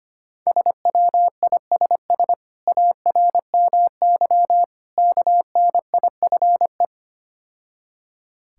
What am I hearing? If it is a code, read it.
Morse code, heard as a keyed tone, about 26 wpm, 700 Hz. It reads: SWISS ARMY KNIFE
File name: practice / 25wpm / Difficult Words - Speed-Racing with Hints - 25wpm